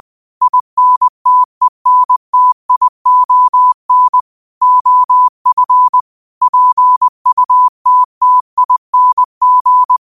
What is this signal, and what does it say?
Morse code, heard as a keyed tone, about 20 wpm, 1000 Hz. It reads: INTENTION OF PUTTING